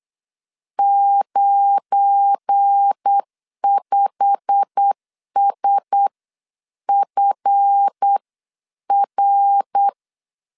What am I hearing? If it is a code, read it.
Morse code, heard as a keyed tone, about 8 wpm, spaced as for 7 wpm, 800 Hz. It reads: 95SFR